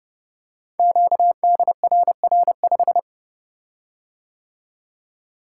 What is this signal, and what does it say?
Morse code, heard as a keyed tone, about 29 wpm, 700 Hz. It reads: QDRR5
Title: Code Groups - 5 Letter:Numbers - 30wpm